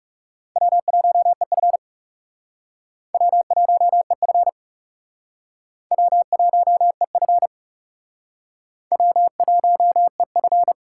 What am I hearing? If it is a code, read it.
Morse code, heard as a keyed tone, about 37 wpm, 700 Hz. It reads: W1EF W1EF W1EF W1EF
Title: Speed Racing - USA Call Signs 30wpm